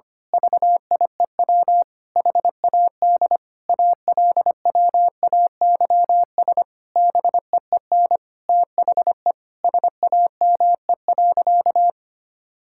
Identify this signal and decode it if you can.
Morse code, heard as a keyed tone, about 25 wpm, 700 Hz. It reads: VIEW HAD ALWAYS BEEN THE SAME.
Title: Sentences from Top 1000 Words - Review Part 6 - 25wpm